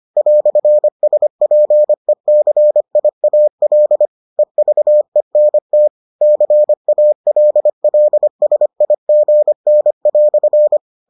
Morse code, heard as a keyed tone, about 25 wpm, 600 Hz. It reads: "SPECIAL EVENT CALLSIGN"